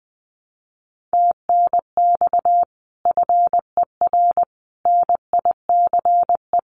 Morse code, heard as a keyed tone, about 20 wpm, 700 Hz. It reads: TNX FER NICE
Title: Common QSO Phrases Encore 20wpm